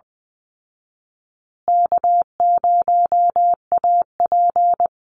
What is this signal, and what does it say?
Morse code, heard as a keyed tone, about 20 wpm, 700 Hz. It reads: K0AP
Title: Easy Call Signs Repeated 3x - 20wpm